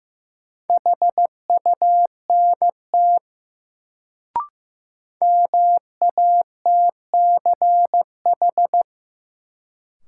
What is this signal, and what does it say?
Morse code, heard as a keyed tone, about 15 wpm, 700 Hz. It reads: HUNT MATCH
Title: Sentences from Top 1600 Words - New Word Drill - 15wpm